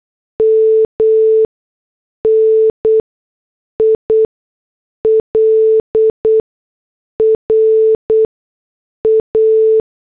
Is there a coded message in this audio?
Yes: Morse code from 0.4 to 9.8 s, heard as a keyed tone, about 8 wpm, spaced as for 6 wpm, 440 Hz. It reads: MNILRA